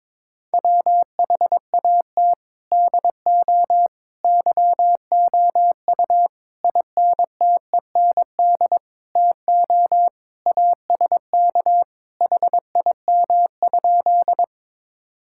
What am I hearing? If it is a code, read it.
Morse code, heard as a keyed tone, about 22 wpm, 700 Hz. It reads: WHAT DO YOU INTEND TO ASK HIM?
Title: Sentences from Top 1600 Words 22wpm